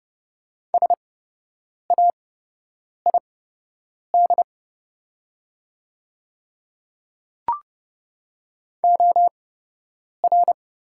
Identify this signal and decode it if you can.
Morse code, heard as a keyed tone, about 30 wpm, spaced as for 8 wpm, 700 Hz. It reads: SAID OR
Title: 8x Character Spacing - Top 100 Words - 30wpm